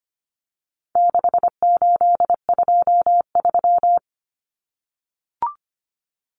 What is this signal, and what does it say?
Morse code, heard as a keyed tone, about 25 wpm, 700 Hz. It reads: 6823
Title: Numbers - 4-Digit 25wpm